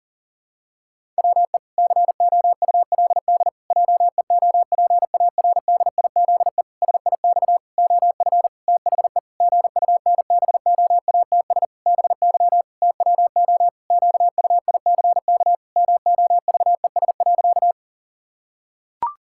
Morse code, heard as a keyed tone, about 40 wpm, 700 Hz. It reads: WE COULD JEOPARDIZE SIX OF THE GUNBOATS BY TWO QUICK MOVES.
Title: Pangram Challenge v2 - 40wpm